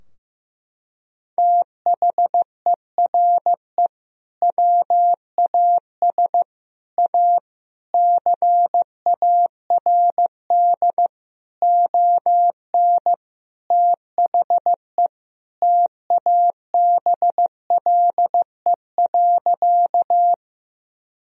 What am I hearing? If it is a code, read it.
Morse code, heard as a keyed tone, about 15 wpm, 700 Hz. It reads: THERE WAS A CARD ON THE TABLE.
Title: Sentences from Top 1000 Words - Review Part 4 - 15wpm